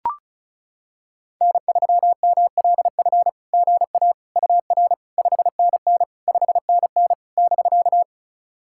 Morse code, heard as a keyed tone, about 34 wpm, 690 Hz. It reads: N3MLF GA UR 5NN 5NN <BK>
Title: SOTA QSO Practice 35wpm